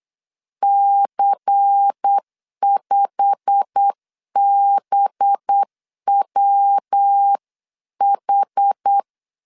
Morse code, heard as a keyed tone, about 8 wpm, 800 Hz. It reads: C5BWH
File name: 8wpm